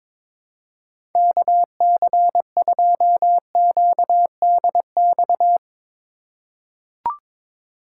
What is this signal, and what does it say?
Morse code, heard as a keyed tone, about 22 wpm, 700 Hz. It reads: KC2QDX
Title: USA Call Signs - 6 Characters - 4x Character Spacing - 22wpm